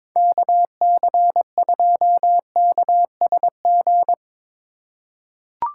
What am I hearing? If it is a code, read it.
Morse code, heard as a keyed tone, about 21 wpm, 700 Hz. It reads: KC2KSG
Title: USA Call Signs - 6 Characters - 8x Character Spacing - 22wpm